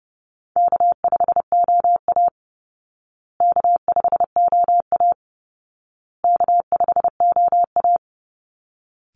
Morse code, heard as a keyed tone, about 30 wpm, 700 Hz. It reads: K5OA K5OA K5OA
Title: Call Signs Repeated 2x - 30wpm